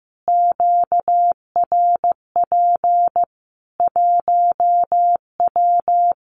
Morse code, heard as a keyed tone, about 15 wpm, 700 Hz. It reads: QRP 1W
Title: Common QSO Phrases 15wpm